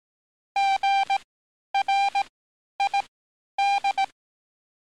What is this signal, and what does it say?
Morse code, heard as a keyed tone, about 18 wpm, spaced as for 10 wpm, 790 Hz. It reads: GRID
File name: training / NAVPERS 11049to11060 / NAVPERS 11050 RA RB